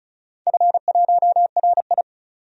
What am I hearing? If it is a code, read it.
Morse code, heard as a keyed tone, about 36 wpm, 700 Hz. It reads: F1RI
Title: DXCC-100 Call Signs - 4 Characters - 35wpm